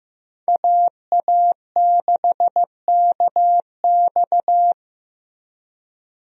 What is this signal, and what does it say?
Morse code, heard as a keyed tone, about 15 wpm, 700 Hz. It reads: AA6KX
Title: USA Call Signs - 5 Characters - Repeated 2x - 15wpm